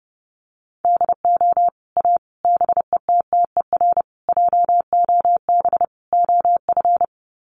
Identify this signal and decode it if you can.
Morse code, heard as a keyed tone, about 30 wpm, 700 Hz. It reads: DO A BETTER JOB OF
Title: Sets of 5 Words Encore2 30wpm